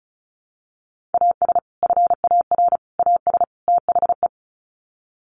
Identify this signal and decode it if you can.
Morse code, heard as a keyed tone, about 34 wpm, 700 Hz. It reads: AS FAR AS THE
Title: Sets of 4 Words 35wpm